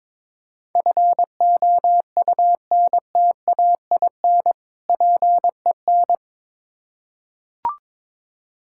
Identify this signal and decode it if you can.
Morse code, heard as a keyed tone, about 22 wpm, 700 Hz. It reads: FOUNTAIN PEN
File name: Difficult Words with Hints - 22wpm